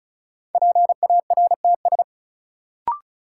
Morse code, heard as a keyed tone, about 36 wpm, 700 Hz. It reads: PARTS
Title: Sets of 3 Words - New Word Drill - 35wpm